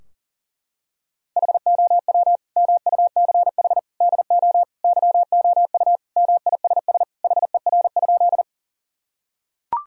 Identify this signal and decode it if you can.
Morse code, heard as a keyed tone, about 40 wpm, 700 Hz. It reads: HOW MUCH DO YOU MISS HER?
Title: Sentences from Top 1000 Words - Review Part 3 - 40wpm